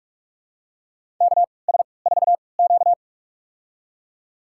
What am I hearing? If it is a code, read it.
Morse code, heard as a keyed tone, about 44 wpm, spaced as for 25 wpm, 690 Hz. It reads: KS4Q
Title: USA Call Signs - 4 Characters - 3x Character Spacing - 45wpm